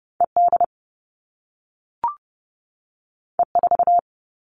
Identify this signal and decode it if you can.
Morse code, heard as a keyed tone, about 30 wpm, 700 Hz. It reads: E D E 4
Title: All Cut Numbers - 2-Digit 30wpm